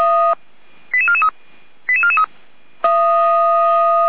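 Five-tone selective call (ZVEI), heard as quick runs of five tones, tones of about 80 ms. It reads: 88443 88443
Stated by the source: ZVEI five-tone call received on an SDR.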